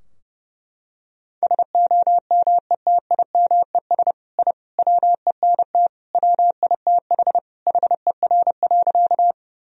Morse code, heard as a keyed tone, about 32 wpm, 700 Hz. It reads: SOMETIMES I WENT WITH HER.
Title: Sentences from Top 1000 Words - Review Part 3 - 30wpm